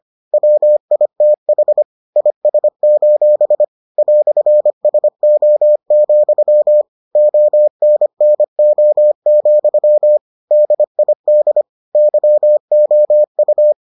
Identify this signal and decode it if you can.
Morse code, heard as a keyed tone, about 25 wpm, 600 Hz. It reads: WITH IS: "SO, ONNO, DID YOU